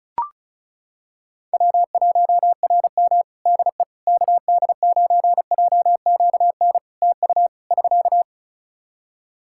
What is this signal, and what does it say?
Morse code, heard as a keyed tone, about 34 wpm, 700 Hz. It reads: W1RM DE KD9JQN TU <SK>
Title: K1USN SST QSO Practice 35wpm